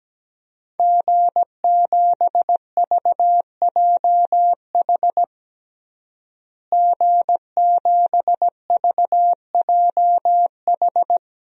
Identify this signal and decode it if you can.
Morse code, heard as a keyed tone, about 17 wpm, 700 Hz. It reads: G7VJH G7VJH
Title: DXCC-100 Call Signs - 5 Characters - Repeated 3x - 17wpm